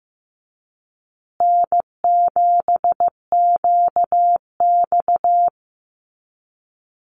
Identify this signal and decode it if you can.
Morse code, heard as a keyed tone, about 15 wpm, 700 Hz. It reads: N7QX
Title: USA Call Signs Repeated 3x 15wpm